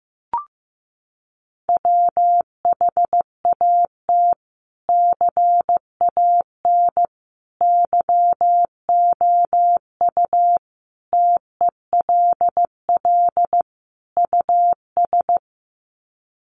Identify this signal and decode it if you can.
Morse code, heard as a keyed tone, about 15 wpm, 700 Hz. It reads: WHAT CAN YOU TELL US
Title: Sets of 5 Words Encore 15wpm